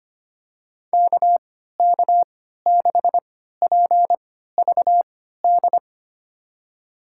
Morse code, heard as a keyed tone, about 25 wpm, spaced as for 14 wpm, 700 Hz. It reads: KK6PVD
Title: USA Call Signs - 6 Characters - 3x Character Spacing - 25wpm